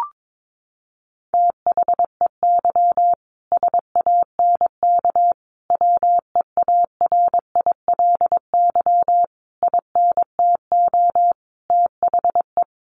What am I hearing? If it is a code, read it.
Morse code, heard as a keyed tone, about 22 wpm, 700 Hz. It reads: THEY SANK WEARILY INTO THE
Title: A Sound Of Thunder 22wpm